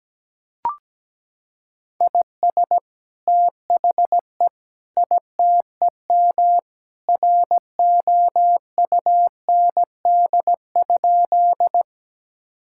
Morse code, heard as a keyed tone, about 17 wpm, 700 Hz. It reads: IS THE ITEM ROUND?
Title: Sentences from Top 1000 Words - Review Part 6 - 17wpm